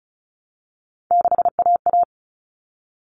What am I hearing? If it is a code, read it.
Morse code, heard as a keyed tone, about 34 wpm, 700 Hz. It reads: 6AA
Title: Common Cut Numbers - 3-Digit 35wpm